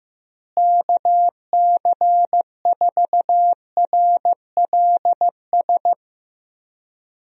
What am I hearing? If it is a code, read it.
Morse code, heard as a keyed tone, about 15 wpm, 700 Hz. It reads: KC4RLS
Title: USA Call Signs - 6 Characters - Repeated 2x - 15wpm